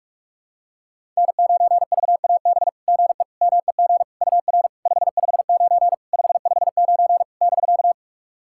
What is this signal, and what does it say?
Morse code, heard as a keyed tone, about 44 wpm, 690 Hz. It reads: N9VAB GE MEG UR 559 559 <BK>
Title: SOTA QSO Practice 45wpm